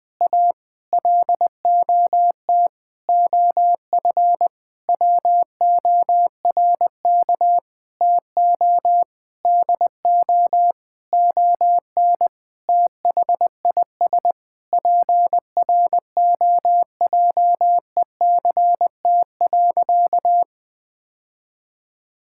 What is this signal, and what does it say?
Morse code, heard as a keyed tone, about 20 wpm, 700 Hz. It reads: A LOT OF WORK TO DO ON THIS PROJECT.
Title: Sentences with 6-Word Phrases - 20wpm